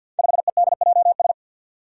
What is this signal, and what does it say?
Morse code, heard as a keyed tone, about 51 wpm, 700 Hz. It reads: 5EDJS
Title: Five Challenging Letter-Number Combinations 50wpm